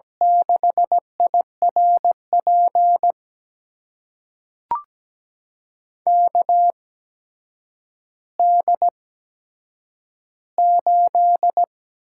Morse code, heard as a keyed tone, about 17 wpm, spaced as for 8 wpm, 700 Hz. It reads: 6IRP KD8
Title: USA Call Signs - 6 Characters - 8x Character Spacing - 17wpm